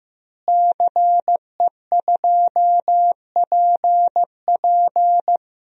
Morse code, heard as a keyed tone, about 15 wpm, 700 Hz. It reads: CE2PP
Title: DXCC-100 Call Signs - 5 Characters - Repeated 3x - 15wpm